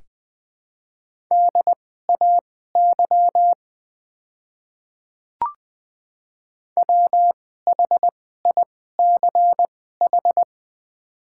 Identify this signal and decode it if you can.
Morse code, heard as a keyed tone, about 20 wpm, spaced as for 14 wpm, 700 Hz. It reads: DAY WHICH
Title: 2x Character Spacing - Top 100 Words - 20wpm